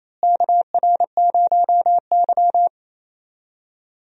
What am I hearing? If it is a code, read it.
Morse code, heard as a keyed tone, about 28 wpm, 700 Hz. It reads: KR0Y
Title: USA Call Signs - 4 Characters - Repeated 2x - 28wpm